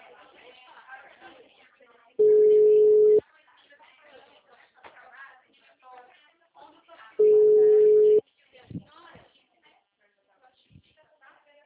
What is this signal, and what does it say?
Ringback tone (Europe, 425 Hz), with speech.
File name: out- -2003- - - .51498